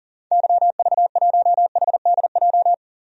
Morse code, heard as a keyed tone, about 40 wpm, 700 Hz. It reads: YV1HDJ
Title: DXCC-100 Call Signs - 6 Characters - 40wpm